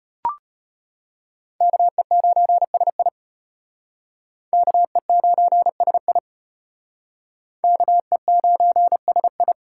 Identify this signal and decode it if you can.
Morse code, heard as a keyed tone, about 33 wpm, 700 Hz. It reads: KE9SI KE9SI KE9SI